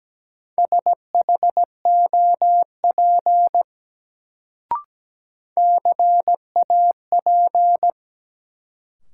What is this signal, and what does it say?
Morse code, heard as a keyed tone, about 17 wpm, 700 Hz. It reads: SHOP CAP
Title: Sentences from Top 1600 Words - New Word Drill Alt - 17wpm